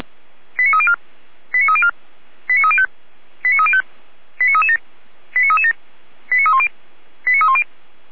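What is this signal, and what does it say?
Five-tone selective call (ZVEI), heard as quick runs of five tones, tones of about 70 ms. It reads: 89304 89304 89305 89305 89337 89337 89310 89310
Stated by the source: ZVEI five-tone selective call received on SDR.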